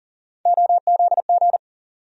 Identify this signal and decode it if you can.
Morse code, heard as a keyed tone, about 40 wpm, 700 Hz. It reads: OZG
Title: USA Call Sign Postfixes - 3 Letters 40wpm